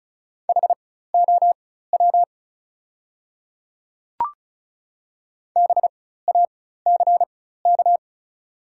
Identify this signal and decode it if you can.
Morse code, heard as a keyed tone, about 36 wpm, spaced as for 16 wpm, 700 Hz. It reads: HOW BACK
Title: 4x Character Spacing - Top 200 Words - 35wpm